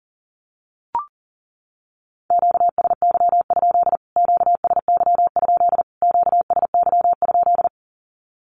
Morse code, heard as a keyed tone, about 37 wpm, 700 Hz. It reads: QSY? QSY? QSY?
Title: Q-Codes Repeated 3x 40wpm